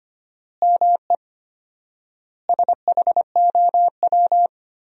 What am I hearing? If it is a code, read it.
Morse code, heard as a keyed tone, about 25 wpm, 700 Hz. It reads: ME SHOW
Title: 4x Word-Spacing - Sentences from Top 500 Words - 25wpm